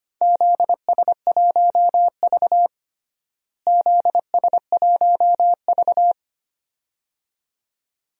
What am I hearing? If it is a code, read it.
Morse code, heard as a keyed tone, about 25 wpm, 700 Hz. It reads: ZS1V ZS1V
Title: DXCC-100 Call Signs - 4 Characters - Repeated 2x - 25wpm